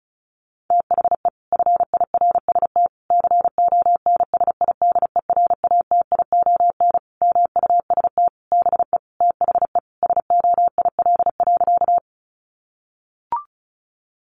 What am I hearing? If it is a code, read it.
Morse code, heard as a keyed tone, about 34 wpm, 700 Hz. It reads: THE FIRST CONSIDERATION MUST BE THE SOIL.
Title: Sentences from Top 1500 Words 35wpm